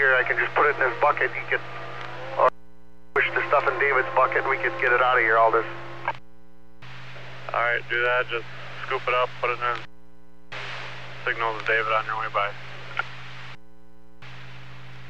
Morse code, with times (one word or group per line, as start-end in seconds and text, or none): none